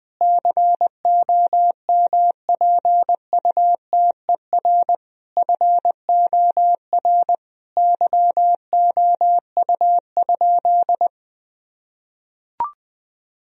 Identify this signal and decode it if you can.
Morse code, heard as a keyed tone, about 20 wpm, 700 Hz. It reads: COMPUTER FOR YOU?
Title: Sentences from Top 1000 Words - Review Part 3 - 20wpm